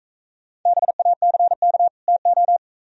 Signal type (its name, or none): Morse code